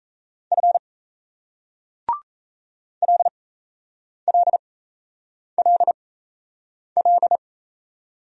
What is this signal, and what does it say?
Morse code, heard as a keyed tone, about 39 wpm, spaced as for 8 wpm, 700 Hz. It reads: F LLLL